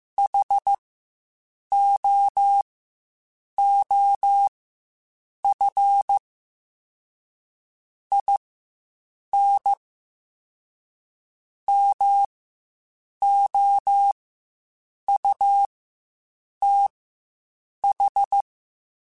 Morse code, heard as a keyed tone, about 15 wpm, spaced as for 7 wpm, 800 Hz. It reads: HOOF IN MOUTH
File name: Quotes Spaced